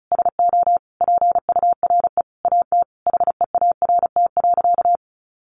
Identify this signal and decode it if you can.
Morse code, heard as a keyed tone, about 34 wpm, 700 Hz. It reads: SO PURE AT HEART.
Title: Sentences from Top 1500 Words 35wpm